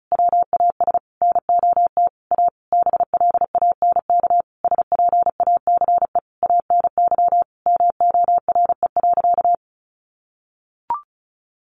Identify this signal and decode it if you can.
Morse code, heard as a keyed tone, about 34 wpm, 700 Hz. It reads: WAS NOT A BLANK SPACE ANY MORE.
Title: Sentences from Top 1500 Words 35wpm